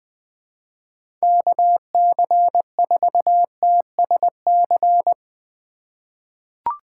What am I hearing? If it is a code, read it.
Morse code, heard as a keyed tone, about 20 wpm, 700 Hz. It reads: KC4TSC
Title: USA Call Signs - 6 Characters - 4x Character Spacing - 20wpm